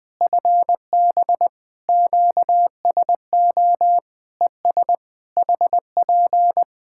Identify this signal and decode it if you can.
Morse code, heard as a keyed tone, about 20 wpm, 700 Hz. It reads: FB QSO ES HP
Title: Practice QSO - Exchange 4 - Ending - 20wpm